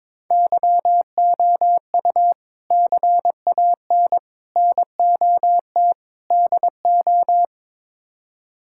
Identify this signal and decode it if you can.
Morse code, heard as a keyed tone, about 22 wpm, 700 Hz. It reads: YOU CAN NOT DO
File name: practice / 22wpm / Build A Phrase - 5-Word Phrases - 22wpm